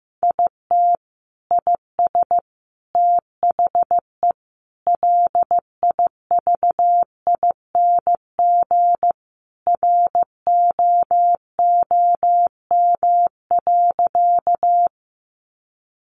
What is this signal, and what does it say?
Morse code, heard as a keyed tone, about 15 wpm, 700 Hz. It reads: IT IS THE LIVING ROOM.
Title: Sentences from Top 1100 Words 15wpm